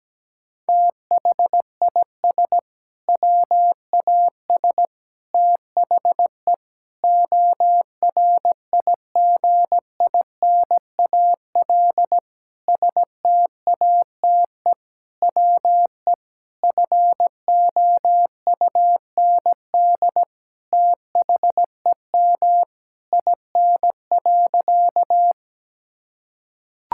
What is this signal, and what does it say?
Morse code, heard as a keyed tone, about 17 wpm, 700 Hz. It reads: THIS WAS THE ORIGINAL STATE WE FOUND THEM IN.
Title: Sentences from Top 1000 Words - Review Part 6 - 17wpm